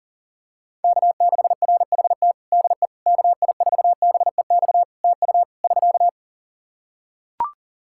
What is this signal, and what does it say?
Morse code, heard as a keyed tone, about 37 wpm, 700 Hz. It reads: K6RHT DE KI4BEX TU <SK>
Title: K1USN SST QSO Practice 40wpm